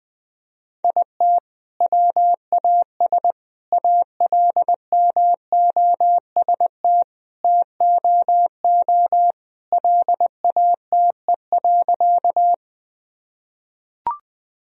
Morse code, heard as a keyed tone, about 20 wpm, 700 Hz. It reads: IT WAS ALMOST TOO LATE.
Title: Sentences from Top 1000 Words - Review Part 4 - 20wpm